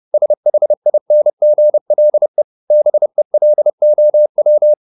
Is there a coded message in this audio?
Yes: Morse code from 0.1 to 4.7 s, heard as a keyed tone, about 30 wpm, 600 Hz. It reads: SHINGLE BELOW